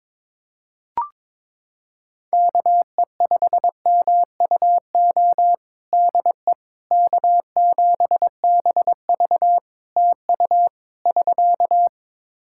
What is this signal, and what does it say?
Morse code, heard as a keyed tone, about 22 wpm, 700 Hz. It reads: KE5MUO DE K7BV TU <SK>
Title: K1USN SST QSO Practice 22wpm